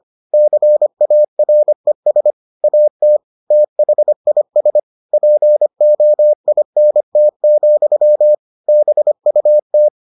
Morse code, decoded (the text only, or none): CARES AT THIS POINT, BUT